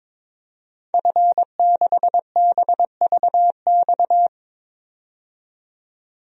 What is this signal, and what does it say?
Morse code, heard as a keyed tone, about 22 wpm, 700 Hz. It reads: F6BVX
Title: DXCC-100 Call Signs - 5 Characters - 22wpm